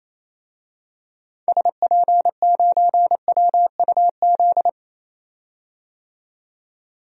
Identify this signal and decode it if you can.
Morse code, heard as a keyed tone, about 28 wpm, 700 Hz. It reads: SP9WUZ
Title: DXCC-100 Call Signs - 6 Characters - 28wpm